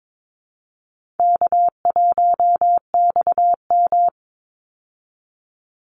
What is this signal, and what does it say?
Morse code, heard as a keyed tone, about 22 wpm, 700 Hz. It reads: K1XM
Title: Call Signs Repeated 3x - 22wpm